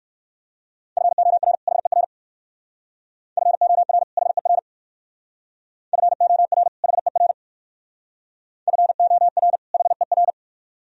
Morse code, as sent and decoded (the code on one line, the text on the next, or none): ..-. --- .-. / .... .-. / ..-. --- .-. / .... .-. / ..-. --- .-. / .... .-. / ..-. --- .-. / .... .-.
FOR HER FOR HER FOR HER FOR HER